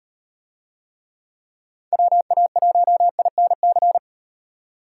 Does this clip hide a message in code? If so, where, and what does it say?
Yes: Morse code from 1.9 to 4.0 s, heard as a keyed tone, about 38 wpm, 700 Hz. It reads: WA1INC